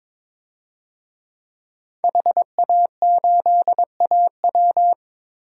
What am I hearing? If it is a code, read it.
Morse code, heard as a keyed tone, about 22 wpm, 700 Hz. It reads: HA8AW
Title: DXCC-100 Call Signs - 5 Characters - Repeated 3x - 22wpm